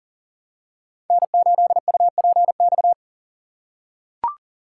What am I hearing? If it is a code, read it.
Morse code, heard as a keyed tone, about 40 wpm, 700 Hz. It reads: N8UPX
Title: USA Call Signs - 5 Characters - 4x Character Spacing - 40wpm